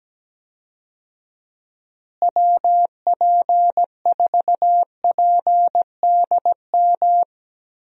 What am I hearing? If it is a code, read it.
Morse code, heard as a keyed tone, about 17 wpm, 700 Hz. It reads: WP4PDM
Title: USA Call Signs - 6 Characters - Repeated 3x - 17wpm